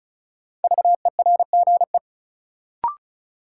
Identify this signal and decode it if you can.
Morse code, heard as a keyed tone, about 36 wpm, 700 Hz. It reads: VERGE
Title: Sets of 4 Words - New Word Drill - 35wpm